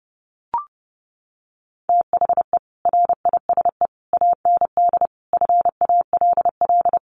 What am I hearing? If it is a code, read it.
Morse code, heard as a keyed tone, about 30 wpm, 700 Hz. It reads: THE RISE AND FALL
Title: Sets of 5 Words Encore2 30wpm